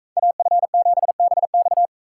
Morse code, heard as a keyed tone, about 39 wpm, 700 Hz. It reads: AF7BX